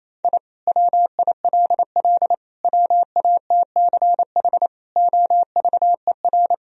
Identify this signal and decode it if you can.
Morse code, heard as a keyed tone, about 28 wpm, 700 Hz. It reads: I WILL WATCH OVER
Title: Sentences from Top 1000 Words - Review Part 1 - 28wpm